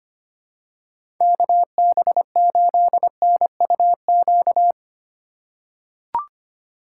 Morse code, heard as a keyed tone, about 25 wpm, 700 Hz. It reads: KB8NUQ
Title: USA Call Signs - 6 Characters - 4x Character Spacing - 25wpm